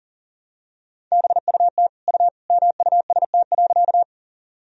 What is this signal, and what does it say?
Morse code, heard as a keyed tone, about 40 wpm, 700 Hz. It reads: BUT U MUST.
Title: Sentences from Top 200 Words and Abbreviations 40wpm